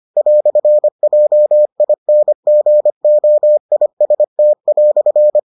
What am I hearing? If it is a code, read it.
Morse code, heard as a keyed tone, about 25 wpm, 600 Hz. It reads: "JINGOIST"